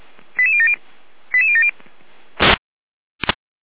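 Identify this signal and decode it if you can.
Five-tone selective call (ZVEI), heard as quick runs of five tones, tones of about 70 ms. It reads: 88070 88070
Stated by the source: ZVEI five-tone selective call received with an SDR.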